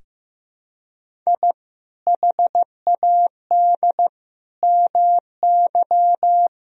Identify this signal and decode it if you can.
Morse code, heard as a keyed tone, about 15 wpm, 700 Hz. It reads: I HAD MY
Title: Sentences from Top 1000 Words - Review Part 1 - 15wpm